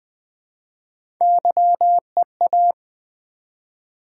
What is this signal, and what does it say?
Morse code, heard as a keyed tone, about 20 wpm, 700 Hz. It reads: YEA
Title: Build A Word - 1 Letter At A Time - 4-Letter Words - 20wpm